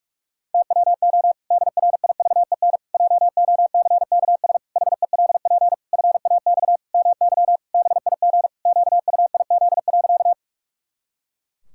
Morse code, heard as a keyed tone, about 46 wpm, 700 Hz. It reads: TWO DRIVEN JOCKS HELP FAX MY BIG QUIZ.
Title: Pangram Challenge v2 - 45wpm